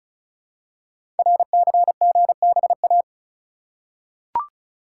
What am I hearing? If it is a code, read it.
Morse code, heard as a keyed tone, about 36 wpm, 700 Hz. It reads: RCGBA
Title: Code Groups - 5 Letters - 35wpm